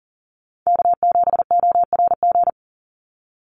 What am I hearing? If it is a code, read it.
Morse code, heard as a keyed tone, about 40 wpm, 700 Hz. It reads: K7ORG
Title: NA SOTA Chaser Callsigns 40wpm